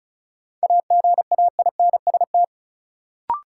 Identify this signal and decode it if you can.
Morse code, heard as a keyed tone, about 34 wpm, 700 Hz. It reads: AGAINST
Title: Sets of 2 Words - New Word Drill - 35wpm